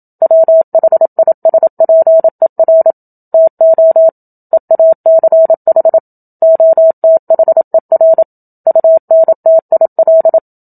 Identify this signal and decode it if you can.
Morse code, heard as a keyed tone, about 27 wpm, 650 Hz. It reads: WHISPER TO EACH OTHER UNTIL
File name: Book Two / 01-Chapter 1 - 27 WPM